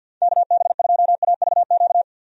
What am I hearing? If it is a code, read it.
Morse code, heard as a keyed tone, about 47 wpm, 700 Hz. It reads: KB2AVQ
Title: USA Call Signs - 6 Characters - 50wpm